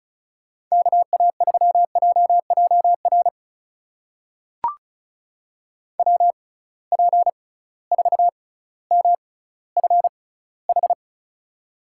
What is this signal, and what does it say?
Morse code, heard as a keyed tone, about 36 wpm, spaced as for 18 wpm, 700 Hz. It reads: KA3JJR WP4MFH
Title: USA Call Signs - 6 Characters - 6x Character Spacing - 35wpm